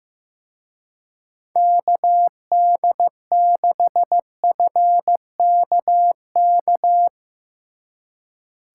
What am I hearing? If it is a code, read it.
Morse code, heard as a keyed tone, about 16 wpm, 700 Hz. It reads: KD6FKK
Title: Speed Racing - USA 2x3 Call Signs 15wpm